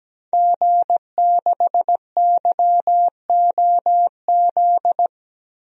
Morse code, heard as a keyed tone, about 17 wpm, 700 Hz. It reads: G6YOZ
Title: DXCC-100 Call Signs - 5 Characters - Repeated 2x - 17wpm